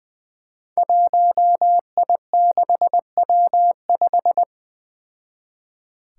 Morse code, heard as a keyed tone, about 20 wpm, 700 Hz. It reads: 1I6W5
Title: Five Challenging Letter-Number Combinations 20wpm